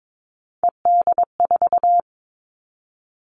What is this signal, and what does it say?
Morse code, heard as a keyed tone, about 22 wpm, 700 Hz. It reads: ED4
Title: All Cut Numbers - 3-Digit 22wpm